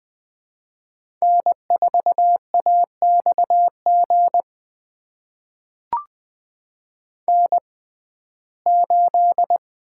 Morse code, heard as a keyed tone, about 20 wpm, spaced as for 14 wpm, 700 Hz. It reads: N4AXG N8
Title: USA Call Signs - 5 Characters - 6x Character Spacing - 20wpm